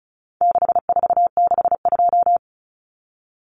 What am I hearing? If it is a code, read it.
Morse code, heard as a keyed tone, about 34 wpm, 700 Hz. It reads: 6462
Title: Numbers - 4-Digit 35wpm